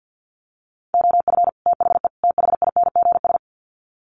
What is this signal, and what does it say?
Morse code, heard as a keyed tone, about 50 wpm, 700 Hz. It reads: OF THE THINGS
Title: Sets of 3 Words 50wpm